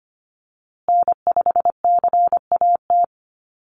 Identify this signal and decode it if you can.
Morse code, heard as a keyed tone, about 25 wpm, 700 Hz. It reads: N5CAT
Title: USA Call Signs Repeated 2x 25wpm